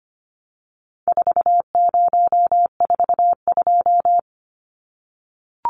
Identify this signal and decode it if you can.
Morse code, heard as a keyed tone, about 25 wpm, 700 Hz. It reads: 4042
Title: Numbers - 4-Digit 25wpm